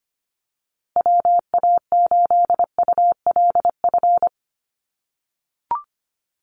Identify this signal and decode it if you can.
Morse code, heard as a keyed tone, about 25 wpm, 700 Hz. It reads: WA8ULF
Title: USA Call Signs Repeated 2x 25wpm